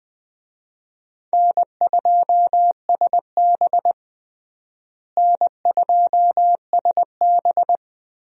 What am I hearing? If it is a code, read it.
Morse code, heard as a keyed tone, about 20 wpm, 700 Hz. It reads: N2SB N2SB
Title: USA Call Signs - 4 Characters - Repeated 2x - 20wpm